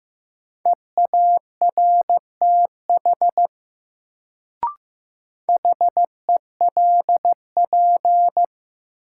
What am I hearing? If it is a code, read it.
Morse code, heard as a keyed tone, about 15 wpm, 700 Hz. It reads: EARTH HELP
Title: Sentences from Top 200 Words - New Word Drill - 15wpm